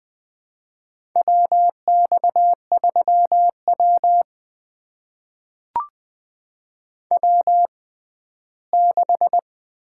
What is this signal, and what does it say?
Morse code, heard as a keyed tone, about 20 wpm, spaced as for 14 wpm, 700 Hz. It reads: WX3W W6
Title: USA Call Signs - 4 Characters - 6x Character Spacing - 20wpm